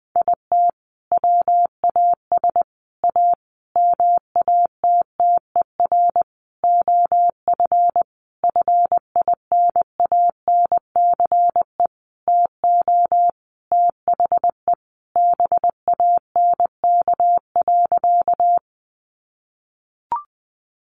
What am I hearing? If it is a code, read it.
Morse code, heard as a keyed tone, about 20 wpm, 700 Hz. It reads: IT WAS A MATTER OF FINANCE TO THE BANK.
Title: Sentences from Top 1500 Words 20wpm